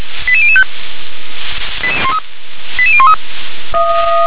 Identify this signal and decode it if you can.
Five-tone selective call (ZVEI), heard as quick runs of five tones, tones of about 70 ms. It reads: 88005 88013
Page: /357/SDR/ZVEI/